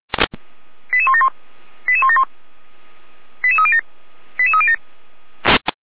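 Five-tone selective call (ZVEI), heard as quick runs of five tones, tones of about 70 ms. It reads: 88171 88171 88307 88307
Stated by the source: ZVEI five-tone selective call received on an SDR.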